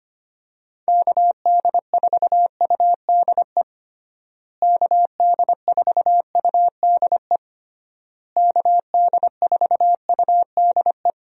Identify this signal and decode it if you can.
Morse code, heard as a keyed tone, about 25 wpm, 700 Hz. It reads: KD4UDE KD4UDE KD4UDE
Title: USA Call Signs - 6 Characters - Repeated 3x - 25wpm